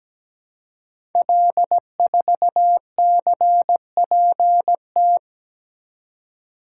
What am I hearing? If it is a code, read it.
Morse code, heard as a keyed tone, about 17 wpm, 700 Hz. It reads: L4CPT
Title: Code Groups - 5 Letter:Numbers - 17wpm